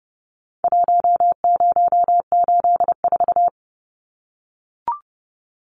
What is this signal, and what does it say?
Morse code, heard as a keyed tone, about 30 wpm, 700 Hz. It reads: 1084
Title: Numbers - Contest Style - 4-Digit 30wpm